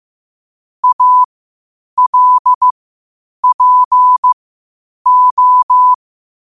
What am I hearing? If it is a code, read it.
Morse code, heard as a keyed tone, about 15 wpm, spaced as for 8 wpm, 1000 Hz. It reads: ALPO
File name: qotdfiles / 8wpm